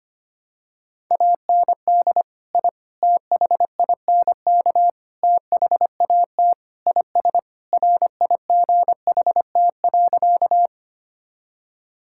Morse code, heard as a keyed tone, about 25 wpm, 700 Hz. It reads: AND I THINK THAT IS RIGHT.
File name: practice / 25wpm / Sentences with 4-Word Phrases - 25wpm